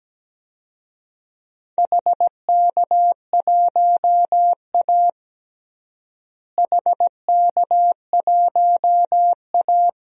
Morse code, heard as a keyed tone, about 17 wpm, 700 Hz. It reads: HK1A HK1A
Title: DXCC-100 Call Signs - 4 Characters - Repeated 2x - 17wpm